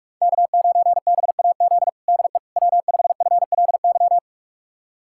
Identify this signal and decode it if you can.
Morse code, heard as a keyed tone, about 46 wpm, 700 Hz. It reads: K9BAZ DE W5FLY
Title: USA Callsign DE Callsign - 5 Characters - 45wpm